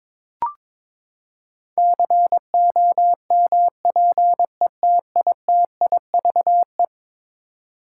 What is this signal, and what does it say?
Morse code, heard as a keyed tone, about 22 wpm, 700 Hz. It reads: COMPETITIVE
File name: practice / 22wpm / Eleven Letter Words - 22wpm